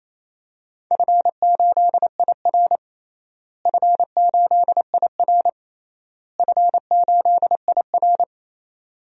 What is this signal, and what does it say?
Morse code, heard as a keyed tone, about 28 wpm, 700 Hz. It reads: F8IR F8IR F8IR
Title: DXCC-100 Call Signs - 4 Characters - Repeated 3x - 28wpm